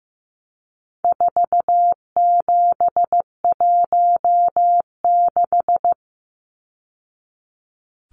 Morse code, heard as a keyed tone, about 15 wpm, 700 Hz. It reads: 4716
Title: Numbers - 4-Digit 15wpm